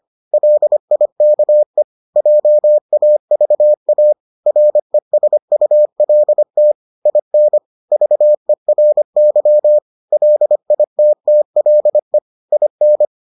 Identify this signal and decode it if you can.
Morse code, heard as a keyed tone, about 25 wpm, 600 Hz. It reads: LIKE JAVA RESULT IN VERY LITTLE IN